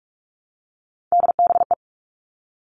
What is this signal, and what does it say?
Morse code, heard as a keyed tone, about 44 wpm, 700 Hz. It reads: DBE
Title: All Cut Numbers - 3-Digit 45wpm